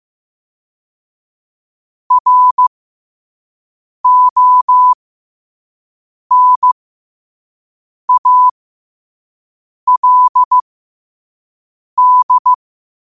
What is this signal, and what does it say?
Morse code, heard as a keyed tone, about 15 wpm, spaced as for 5 wpm, 1000 Hz. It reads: RONALD
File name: qotdfiles / 5wpm